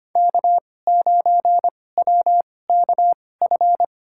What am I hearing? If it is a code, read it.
Morse code, heard as a keyed tone, about 25 wpm, spaced as for 18 wpm, 700 Hz. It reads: K9WKF
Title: USA Call Signs - 5 Characters - 2x Character Spacing - 25wpm